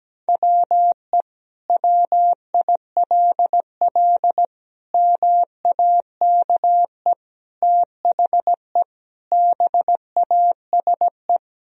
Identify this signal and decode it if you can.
Morse code, heard as a keyed tone, about 17 wpm, 700 Hz. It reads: WE WILL MAKE THE BASE
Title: Sentences from Top 1000 Words - Review Part 1 - 17wpm